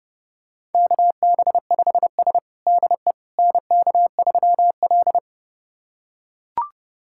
Morse code, heard as a keyed tone, about 30 wpm, 700 Hz. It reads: KB5S DE NK3L
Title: USA Callsign DE Callsign - 4 Characters - 30wpm